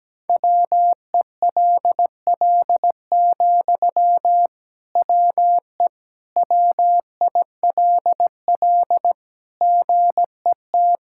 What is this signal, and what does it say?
Morse code, heard as a keyed tone, about 17 wpm, 700 Hz. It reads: WELL, WE WILL GET
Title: Sentences from Top 1000 Words - Review Part 6 - 17wpm